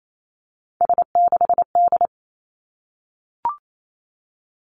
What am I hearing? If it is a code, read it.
Morse code, heard as a keyed tone, about 28 wpm, 700 Hz. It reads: S6D
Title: Three Challenging Letter-Number Combinations 28wpm